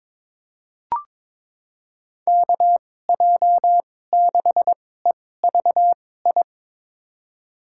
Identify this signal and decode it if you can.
Morse code, heard as a keyed tone, about 22 wpm, spaced as for 16 wpm, 700 Hz. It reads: KJ6EVI
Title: USA Call Signs - 2x Character Spacing - 22wpm